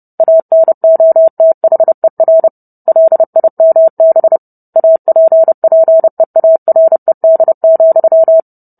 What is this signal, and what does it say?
Morse code, heard as a keyed tone, about 30 wpm, 650 Hz. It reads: ANOTHER LIMB APPEARED,